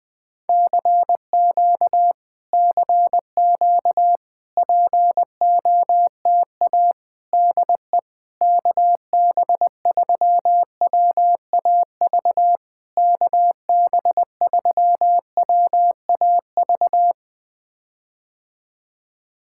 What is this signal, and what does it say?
Morse code, heard as a keyed tone, about 20 wpm, 700 Hz. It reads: CQ CQ POTA DE KB3WAV KB3WAV
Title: POTA QSO Practice - 20wpm